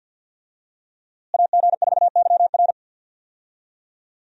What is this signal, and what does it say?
Morse code, heard as a keyed tone, about 52 wpm, 700 Hz. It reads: AG4YR
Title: USA Call Signs - 5 Characters - Repeated 2x - 50wpm